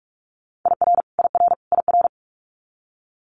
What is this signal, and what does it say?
Morse code, heard as a keyed tone, about 46 wpm, 700 Hz. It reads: IR IR IR
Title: Multi-letter Phonemes Repeated 3x 45wpm